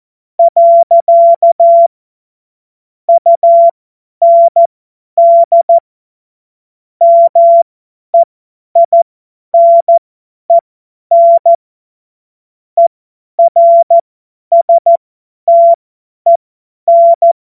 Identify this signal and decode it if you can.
Morse code, heard as a keyed tone, about 14 wpm, spaced as for 10 wpm, 680 Hz. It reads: . UND MEINEN ERSTEN
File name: Morsekurs / Witze / Witz 14.November 2023 50 F70